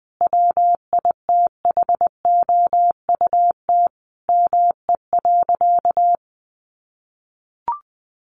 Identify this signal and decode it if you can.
Morse code, heard as a keyed tone, about 20 wpm, 700 Hz. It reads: WITHOUT ME.
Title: Sentences from Top 1200 Words 20wpm